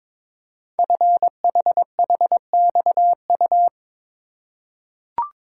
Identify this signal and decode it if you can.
Morse code, heard as a keyed tone, about 23 wpm, 700 Hz. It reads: FHHXU
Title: Code Groups - 5 Letters - 22wpm